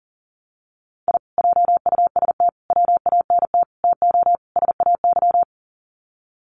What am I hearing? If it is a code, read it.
Morse code, heard as a keyed tone, about 40 wpm, 700 Hz. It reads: I JUST WANT TO SAY
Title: Sets of 5 Words Encore 40wpm